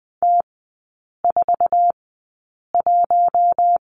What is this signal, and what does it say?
Morse code, heard as a keyed tone, about 20 wpm, spaced as for 8 wpm, 700 Hz. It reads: T41
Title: 138 Lesson 23e - Letter f - Three Characters 20wpm